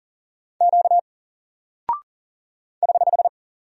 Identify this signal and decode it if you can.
Morse code, heard as a keyed tone, about 38 wpm, 700 Hz. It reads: Q <HH>
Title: Single Character (Letter:Number:Punctuation:Prosign) Alt 40wpm